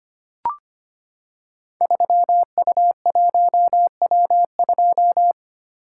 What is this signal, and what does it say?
Morse code, heard as a keyed tone, about 25 wpm, 700 Hz. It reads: 3U1W2
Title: Code Groups - 5 Letter:Numbers - 25wpm